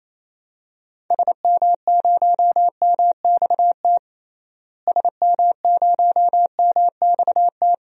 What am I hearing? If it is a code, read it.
Morse code, heard as a keyed tone, about 28 wpm, 700 Hz. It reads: SM0MXT SM0MXT
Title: DXCC-100 Call Signs - 6 Characters - Repeated 3x - 28wpm